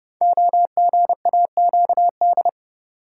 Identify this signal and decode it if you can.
Morse code, heard as a keyed tone, about 30 wpm, 700 Hz. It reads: OGAQD
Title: Code Groups - 5 Letters - 30wpm